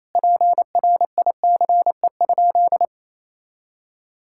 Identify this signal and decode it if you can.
Morse code, heard as a keyed tone, about 28 wpm, 700 Hz. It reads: PRICE?
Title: Sentences from Top 1600 Words 28wpm